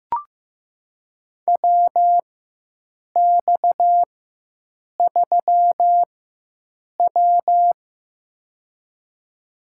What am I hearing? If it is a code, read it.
Morse code, heard as a keyed tone, about 15 wpm, spaced as for 7 wpm, 700 Hz. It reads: WX3W
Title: USA Call Signs - 4 Characters - 4x Character Spacing - 15wpm